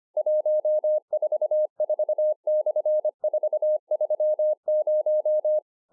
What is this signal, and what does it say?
Morse code, heard as a keyed tone, about 25 wpm, 610 Hz. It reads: 144/430